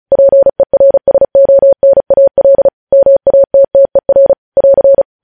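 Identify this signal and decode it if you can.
Morse code, heard as a keyed tone, about 36 wpm, 550 Hz. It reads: PERSONAL MATTER <AR>
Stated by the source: Morse code news headlines at 35.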